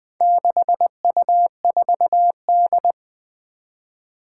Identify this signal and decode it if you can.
Morse code, heard as a keyed tone, about 20 wpm, 700 Hz. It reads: 6U4D
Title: Four Challenging Letter-Number Combinations 20wpm